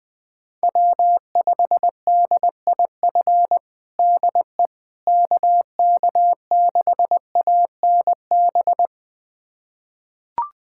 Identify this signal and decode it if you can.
Morse code, heard as a keyed tone, about 20 wpm, 700 Hz. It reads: W5DIF DE KK6ANB
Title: USA Callsign DE Callsign 20wpm